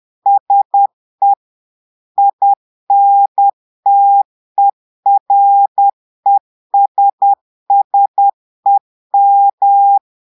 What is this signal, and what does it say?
Morse code, heard as a keyed tone, about 10 wpm, 800 Hz. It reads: SE INTERESSEM
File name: portugais texte en clair 2 vitesse 10 mots minutes